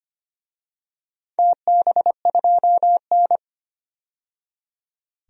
Morse code, heard as a keyed tone, about 25 wpm, 700 Hz. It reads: TB2N
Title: DXCC-100 Call Signs - 4 Characters - 25wpm